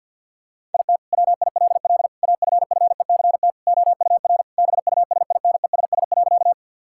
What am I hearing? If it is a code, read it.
Morse code, heard as a keyed tone, about 50 wpm, 700 Hz. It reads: IT WILL AFFECT OUR BUSINESS.
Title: Sentences from Top 1000 Words - Review Part 6 - 50wpm